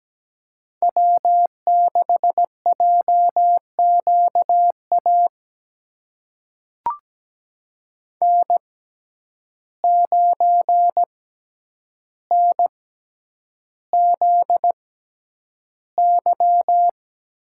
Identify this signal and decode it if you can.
Morse code, heard as a keyed tone, about 17 wpm, spaced as for 9 wpm, 700 Hz. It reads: W6JQA N9NZY
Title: USA Call Signs - 5 Characters - 6x Character Spacing - 17wpm